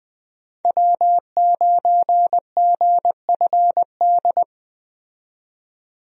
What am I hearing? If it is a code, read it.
Morse code, heard as a keyed tone, about 21 wpm, 700 Hz. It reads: W9GFD
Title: USA Call Signs - 5 Characters - 20wpm